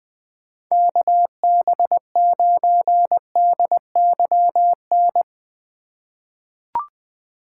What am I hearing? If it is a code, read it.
Morse code, heard as a keyed tone, about 20 wpm, 700 Hz. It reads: KB9DYN
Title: USA Call Signs - 3x Character Spacing - 20wpm